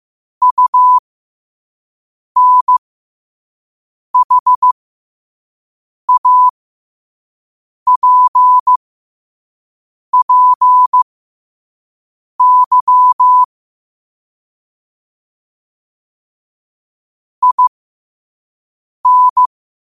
Morse code, heard as a keyed tone, about 15 wpm, spaced as for 5 wpm, 1000 Hz. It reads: UNHAPPY IN